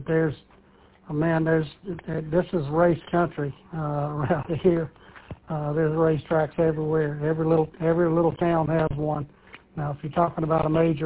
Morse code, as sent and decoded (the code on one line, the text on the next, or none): none